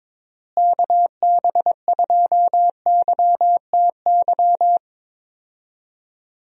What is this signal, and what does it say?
Morse code, heard as a keyed tone, about 22 wpm, 700 Hz. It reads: KB2YTY
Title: USA Call Signs - 6 Characters - Repeated 3x - 22wpm